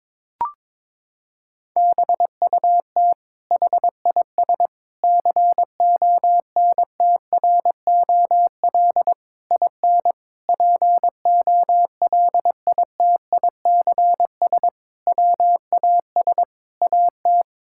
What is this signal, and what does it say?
Morse code, heard as a keyed tone, about 22 wpm, 700 Hz. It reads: BUT HIS CONTROL IN POLITICS WAS AT
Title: Sentences from Top 1000 Words - Review Part 6 - 22wpm